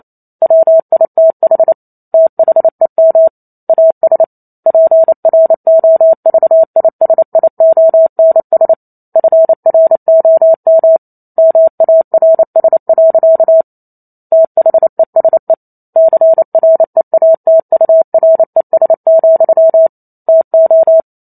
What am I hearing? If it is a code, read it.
Morse code, heard as a keyed tone, about 28 wpm, 650 Hz. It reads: WITH THEM AS PROVISIONS FROM MARS. THESE CREATURES, TO